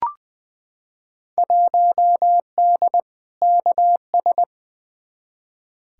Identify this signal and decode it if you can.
Morse code, heard as a keyed tone, about 20 wpm, 700 Hz. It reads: 1D KS
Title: Field Day Exchanges - 20wpm